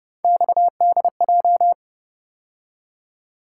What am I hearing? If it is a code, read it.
Morse code, heard as a keyed tone, about 29 wpm, 700 Hz. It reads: XDJ